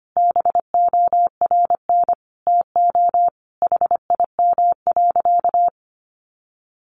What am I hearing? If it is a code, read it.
Morse code, heard as a keyed tone, about 25 wpm, 700 Hz. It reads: BORN TO HIM.
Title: Sentences from Top 1500 Words 25wpm